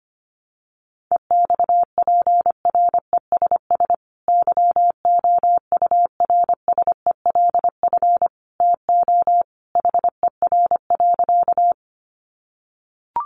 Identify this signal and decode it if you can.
Morse code, heard as a keyed tone, about 25 wpm, 700 Hz. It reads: EXPRESS YOURSELF TO HER.
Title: Sentences from Top 1100 Words 25wpm